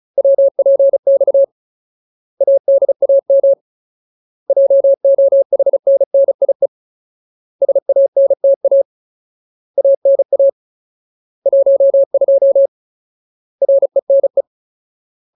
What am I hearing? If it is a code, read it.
Morse code, heard as a keyed tone, about 35 wpm, 550 Hz. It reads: WPX ADAM JOHNNIE SANTA ANA 12 RENE
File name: Ultimate Word List / 35WPM Ultimate